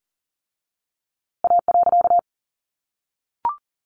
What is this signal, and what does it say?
Morse code, heard as a keyed tone, about 40 wpm, 700 Hz. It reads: A.
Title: A Sound Of Thunder 40wpm